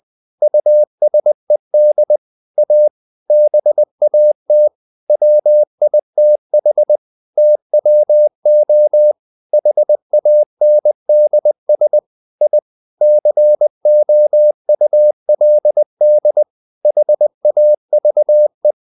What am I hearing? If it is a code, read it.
Morse code, heard as a keyed tone, about 20 wpm, 600 Hz. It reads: USED A BAT WITH TWO HANDS I COULD HAVE
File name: the old man and the sea part 2 20wpm